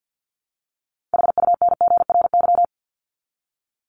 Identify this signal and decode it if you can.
Morse code, heard as a keyed tone, about 50 wpm, 700 Hz. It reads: HUNGRY
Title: Top 1251-1500 Words 50wpm